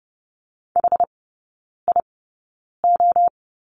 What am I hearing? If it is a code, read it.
Morse code, heard as a keyed tone, about 30 wpm, spaced as for 9 wpm, 700 Hz. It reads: HIO
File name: 081 Lesson 14e - Number 2 - Three Characters 30wpm